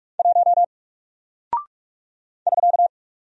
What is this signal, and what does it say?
Morse code, heard as a keyed tone, about 48 wpm, 700 Hz. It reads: 1 <SK>
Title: Single Character (Letter:Number:Punctuation:Prosign) Alt 45wpm